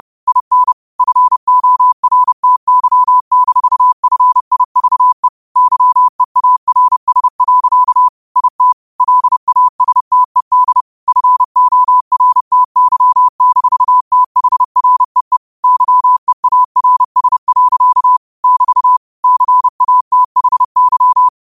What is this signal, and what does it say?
Morse code, heard as a keyed tone, about 30 wpm, 1000 Hz. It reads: IN FORTY-FIVE YEARS. IT LASTED FORTY-THREE YEARS. <BT> CATHY
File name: qotdfiles / 30wpm